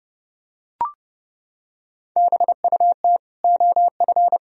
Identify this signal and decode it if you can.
Morse code, heard as a keyed tone, about 30 wpm, 700 Hz. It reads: BUT OF
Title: Sentences from Top 1000 Words - Review Part 2 - 30wpm